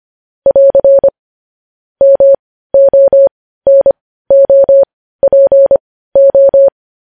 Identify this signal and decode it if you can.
Morse code, heard as a keyed tone, about 25 wpm, spaced as for 15 wpm, 550 Hz. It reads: <AR> MONOPO
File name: podcast / News headlines-1x-25x15